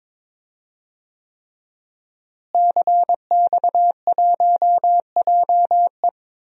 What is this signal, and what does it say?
Morse code, heard as a keyed tone, about 22 wpm, 700 Hz. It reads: CX1JE
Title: DXCC-100 Call Signs - 5 Characters - Repeated 3x - 22wpm